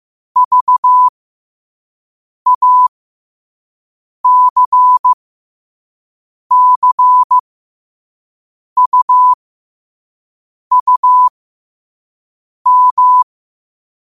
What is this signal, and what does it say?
Morse code, heard as a keyed tone, about 15 wpm, spaced as for 5 wpm, 1000 Hz. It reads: VACCUUM
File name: qotdfiles / 5wpm